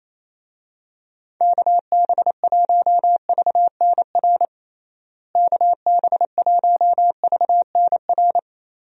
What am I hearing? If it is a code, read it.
Morse code, heard as a keyed tone, about 28 wpm, 700 Hz. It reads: KB1VNR KB1VNR
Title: USA Call Signs - 6 Characters - Repeated 3x - 28wpm